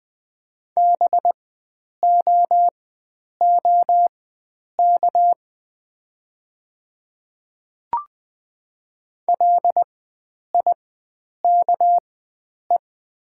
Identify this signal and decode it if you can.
Morse code, heard as a keyed tone, about 20 wpm, spaced as for 9 wpm, 700 Hz. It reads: BOOK LIKE
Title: 4x Character Spacing - Top 300 Words - 20wpm